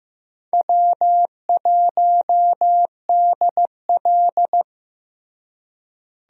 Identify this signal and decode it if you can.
Morse code, heard as a keyed tone, about 15 wpm, 700 Hz. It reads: W1DL
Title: USA Call Signs - 4 Characters - Repeated 2x - 15wpm